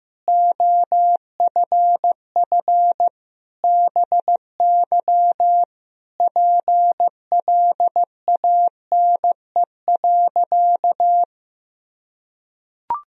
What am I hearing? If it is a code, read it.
Morse code, heard as a keyed tone, about 15 wpm, 700 Hz. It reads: OFF BY PLANE.
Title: Sentences from Top 1000 Words - Review Part 3 - 15wpm